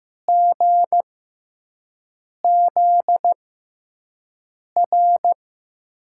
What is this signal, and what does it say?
Morse code, heard as a keyed tone, about 15 wpm, spaced as for 5 wpm, 700 Hz. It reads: GZR
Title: USA Call Signs - 6x Character Spacing - 15wpm